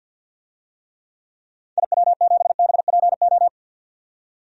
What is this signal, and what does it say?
Morse code, heard as a keyed tone, about 53 wpm, 700 Hz. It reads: IW7BPO
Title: DXCC-100 Call Signs - 6 Characters - 50wpm